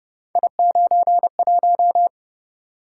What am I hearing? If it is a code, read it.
Morse code, heard as a keyed tone, about 29 wpm, 700 Hz. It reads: I91
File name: Speed-Racing - Three Letter-Number Combinations - 25wpm